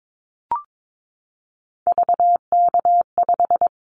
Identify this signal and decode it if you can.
Morse code, heard as a keyed tone, about 22 wpm, 700 Hz. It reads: VK5
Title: Callsign Prefixes for Top 50 DXCC - 22wpm